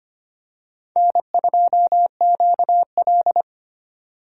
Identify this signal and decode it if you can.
Morse code, heard as a keyed tone, about 25 wpm, 700 Hz. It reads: N2QL
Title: USA Call Signs - 4 Characters - 6x Character Spacing - 25wpm